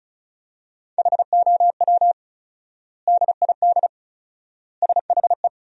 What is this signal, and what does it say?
Morse code, heard as a keyed tone, about 36 wpm, 700 Hz. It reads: HOW DID SHE
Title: 4x Word-Spacing - Sentences from Top 100 Words - 35wpm